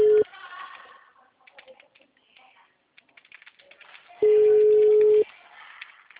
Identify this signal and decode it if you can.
Ringback tone (Europe, 425 Hz).